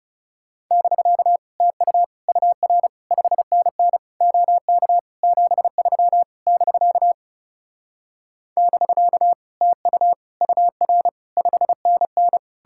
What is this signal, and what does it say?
Morse code, heard as a keyed tone, about 37 wpm, 700 Hz. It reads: <BK> TU UR 5NN OK 73 <BK> <BK> TU UR 5NN
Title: Speed-Racing Common QSO Phrases Encore - 30wpm